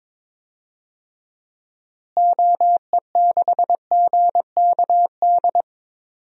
Morse code, heard as a keyed tone, about 22 wpm, 700 Hz. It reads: OE6GKD
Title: DXCC-100 Call Signs - 6 Characters - Repeated 2x - 22wpm